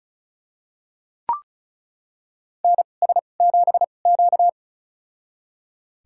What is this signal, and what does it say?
Morse code, heard as a keyed tone, about 34 wpm, spaced as for 25 wpm, 700 Hz. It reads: NS7Q
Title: USA Call Signs - 2x Character Spacing - 35wpm